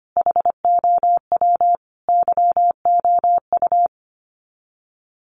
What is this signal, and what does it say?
Morse code, heard as a keyed tone, about 25 wpm, 700 Hz. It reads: HOW YOU
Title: Sets of 5 Words Encore 25wpm